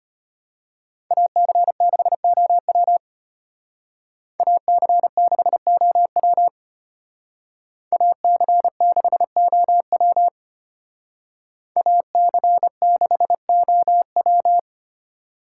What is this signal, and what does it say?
Morse code, heard as a keyed tone, about 31 wpm, 700 Hz. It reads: AC6OW AC6OW AC6OW AC6OW